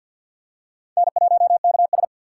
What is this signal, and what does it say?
Morse code, heard as a keyed tone, about 53 wpm, 700 Hz. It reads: N1KS
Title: USA Call Signs - 4 Characters - 3x Character Spacing - 50wpm